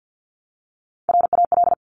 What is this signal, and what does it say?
Morse code, heard as a keyed tone, about 50 wpm, 700 Hz. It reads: RAL
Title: Top Three Letter Combinations in Words 50wpm